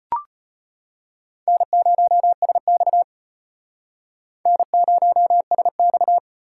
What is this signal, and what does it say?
Morse code, heard as a keyed tone, about 33 wpm, 700 Hz. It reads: N0SX N0SX